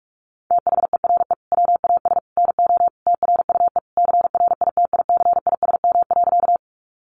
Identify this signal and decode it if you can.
Morse code, heard as a keyed tone, about 44 wpm, 700 Hz. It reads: THERE WAS NO TRUE CRITICISM.
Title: Sentences from Top 1300 Words 45wpm